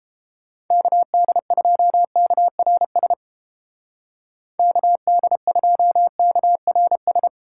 Morse code, heard as a keyed tone, about 32 wpm, 700 Hz. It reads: KD2KRS KD2KRS